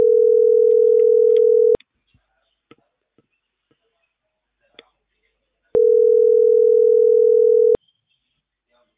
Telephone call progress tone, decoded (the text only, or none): ringback tone